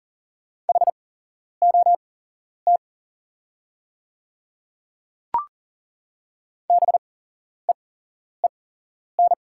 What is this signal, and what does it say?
Morse code, heard as a keyed tone, about 40 wpm, spaced as for 11 wpm, 700 Hz. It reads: HOT BEEN
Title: 8x Character Spacing - Top 100 Words - 40wpm